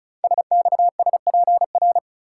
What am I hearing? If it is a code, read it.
Morse code, heard as a keyed tone, about 36 wpm, 700 Hz. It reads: SXSPR
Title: Code Groups - 5 Letters - 35wpm